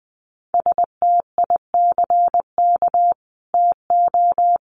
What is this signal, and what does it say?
Morse code, heard as a keyed tone, about 20 wpm, 700 Hz. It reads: STICK TO
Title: Sentences from Top 1200 Words 20wpm